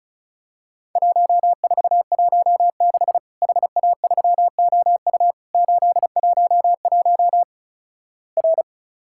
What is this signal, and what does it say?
Morse code, heard as a keyed tone, about 36 wpm, 700 Hz. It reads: 1416 HA3OU 811 R
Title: Worked All Europe DX Contest - QTC Exchange - 35wpm